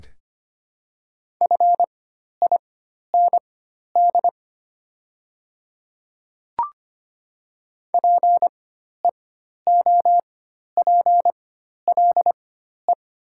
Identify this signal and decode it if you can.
Morse code, heard as a keyed tone, about 25 wpm, spaced as for 12 wpm, 700 Hz. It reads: FIND PEOPLE
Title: 4x Character Spacing - Top 100 Words - 25wpm